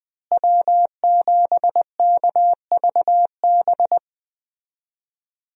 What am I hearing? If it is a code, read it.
Morse code, heard as a keyed tone, about 20 wpm, 700 Hz. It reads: W7KVB
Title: USA Call Signs - 5 Characters - Repeated 3x - 20wpm